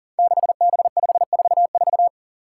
Morse code, heard as a keyed tone, about 40 wpm, 700 Hz. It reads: 6B544
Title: Five Challenging Letter-Number Combinations 40wpm